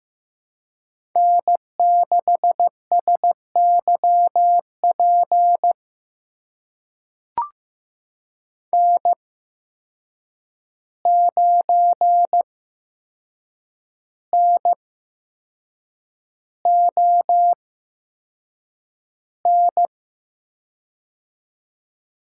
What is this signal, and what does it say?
Morse code, heard as a keyed tone, about 15 wpm, spaced as for 6 wpm, 700 Hz. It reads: N6SYP N9NON
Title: USA Call Signs - 5 Characters - 8x Character Spacing - 15wpm